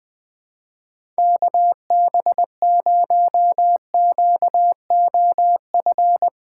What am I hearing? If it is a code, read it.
Morse code, heard as a keyed tone, about 20 wpm, 700 Hz. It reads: KB0QOF
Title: USA Call Signs - 6 Characters - 4x Character Spacing - 20wpm